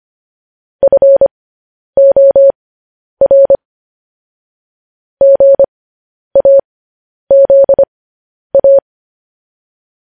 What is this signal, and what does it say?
Morse code, heard as a keyed tone, about 25 wpm, spaced as for 10 wpm, 550 Hz. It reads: FOR GAZA
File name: podcast / News headlines-1x-25x10